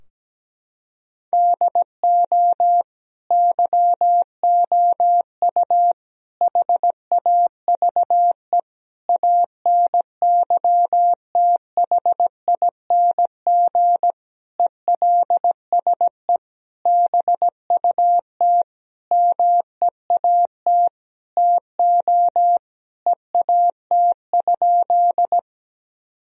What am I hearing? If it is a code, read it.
Morse code, heard as a keyed tone, about 17 wpm, 700 Hz. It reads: DO YOU HAVE ANYTHING ELSE BUT MEAT TO EAT?
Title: Sentences from Top 1000 Words - Review Part 3 - 17wpm